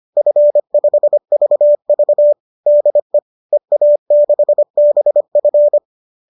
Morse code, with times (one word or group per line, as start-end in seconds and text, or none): 0.2-2.3 F5VV
2.7-3.2 DE
3.5-5.8 EA6BF